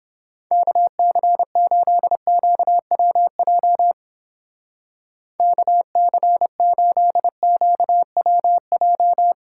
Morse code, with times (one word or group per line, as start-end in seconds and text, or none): 0.5-3.9 KC8QWJ
5.4-9.3 KC8QWJ